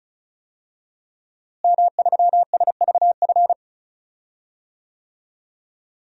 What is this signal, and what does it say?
Morse code, heard as a keyed tone, about 36 wpm, 700 Hz. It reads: M3SVF
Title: DXCC-100 Call Signs - 5 Characters - Repeated 2x - 35wpm